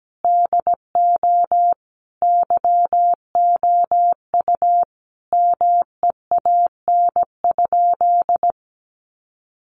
Morse code, heard as a keyed tone, about 17 wpm, 700 Hz. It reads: DO YOU MEAN?
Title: Sentences from Top 900 Words 17wpm